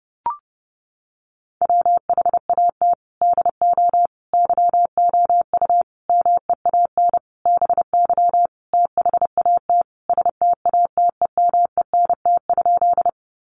Morse code, heard as a keyed tone, about 30 wpm, 700 Hz. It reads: WHAT DO YOU MEAN BY THAT STATEMENT?
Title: Sentences from Top 1200 Words 30wpm